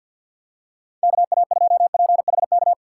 Morse code, heard as a keyed tone, about 52 wpm, 700 Hz. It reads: KA2PHK
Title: USA Call Signs - 6 Characters - 4x Character Spacing - 50wpm